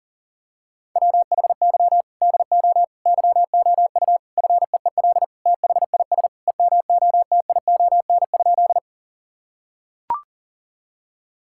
Morse code, heard as a keyed tone, about 40 wpm, 700 Hz. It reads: WHY DO YOU FEEL THIS EMOTION?
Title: Sentences from Top 1600 Words 40wpm